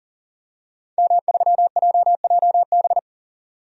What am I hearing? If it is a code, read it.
Morse code, heard as a keyed tone, about 40 wpm, 700 Hz. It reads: M3JJB
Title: DXCC-100 Call Signs - 5 Characters - 40wpm